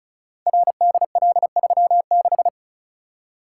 Morse code, heard as a keyed tone, about 36 wpm, 700 Hz. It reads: RDL36
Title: Code Groups - 5 Letter:Numbers - 35wpm